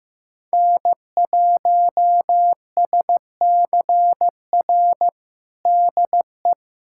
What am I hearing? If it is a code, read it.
Morse code, heard as a keyed tone, about 16 wpm, 700 Hz. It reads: N1SCR DE
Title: USA Callsign DE Callsign - 5 Characters - 15wpm